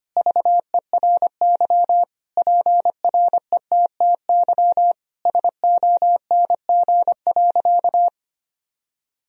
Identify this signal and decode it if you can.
Morse code, heard as a keyed tone, about 25 wpm, 700 Hz. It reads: VERY PRETTY SONG.
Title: Sentences from Top 1000 Words - Review Part 2 - 25wpm